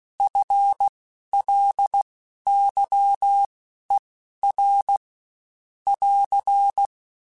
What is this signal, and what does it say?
Morse code, heard as a keyed tone, about 16 wpm, spaced as for 11 wpm, 800 Hz. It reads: FLYER <AR>
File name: Spaced / FL Runs sp mp3s / FL Runs sp @15